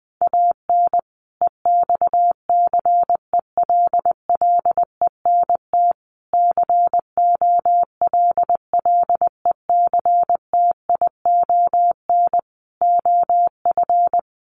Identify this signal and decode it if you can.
Morse code, heard as a keyed tone, about 20 wpm, 700 Hz. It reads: AN EXCELLENT COLLECTION OF
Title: Sentences from Top 1200 Words 20wpm